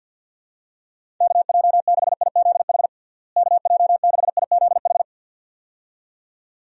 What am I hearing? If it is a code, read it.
Morse code, heard as a keyed tone, about 49 wpm, 700 Hz. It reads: KJ6IZH KJ6IZH
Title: USA Call Signs - 6 Characters - Repeated 2x - 50wpm